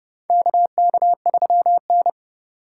Morse code, heard as a keyed tone, about 32 wpm, 700 Hz. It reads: KK3N
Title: USA Call Signs - 4 Characters - 2x Character Spacing - 30wpm